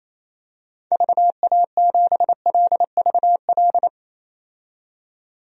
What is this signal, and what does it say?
Morse code, heard as a keyed tone, about 28 wpm, 700 Hz. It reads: VA7LVL
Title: DXCC-100 Call Signs - 6 Characters - 28wpm